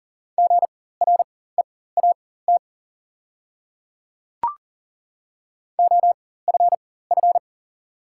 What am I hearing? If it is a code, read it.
Morse code, heard as a keyed tone, about 40 wpm, spaced as for 19 wpm, 690 Hz. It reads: GREAT OFF
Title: 4x Character Spacing - Top 300 Words - 40wpm